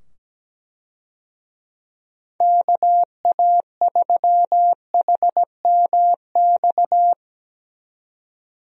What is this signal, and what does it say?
Morse code, heard as a keyed tone, about 17 wpm, 700 Hz. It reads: KA3HMX
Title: USA Call Signs - 6 Characters - 17wpm